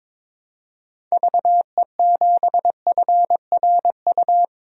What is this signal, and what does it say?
Morse code, heard as a keyed tone, about 22 wpm, 700 Hz. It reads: VE7FRU
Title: DXCC-100 Call Signs - 6 Characters - 22wpm